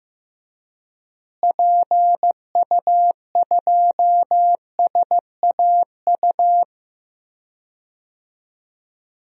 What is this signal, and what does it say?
Morse code, heard as a keyed tone, about 15 wpm, 700 Hz. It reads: PU2SAU
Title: DXCC-100 Call Signs - 6 Characters - Repeated 2x - 15wpm